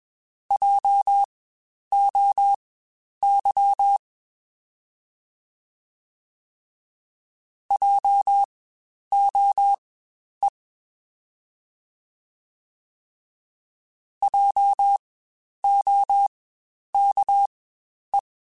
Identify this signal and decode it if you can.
Morse code, heard as a keyed tone, about 21 wpm, spaced as for 10 wpm, 800 Hz. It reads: JOY JOE JOKE
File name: Spaced Alphas / Spaced Alphas mp3s / Js sp @20